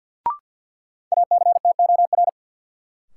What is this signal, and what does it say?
Morse code, heard as a keyed tone, about 53 wpm, 700 Hz. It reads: ACTOR
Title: Sentences from Top 1500 Words - New Word Drill - 50wpm